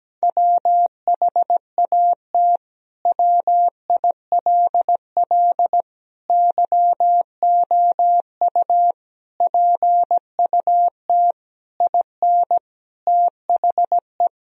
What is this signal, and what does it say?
Morse code, heard as a keyed tone, about 17 wpm, 700 Hz. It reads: WHAT WILL YOU PUT IN THE
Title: Sentences from Top 1000 Words - Review Part 2 - 17wpm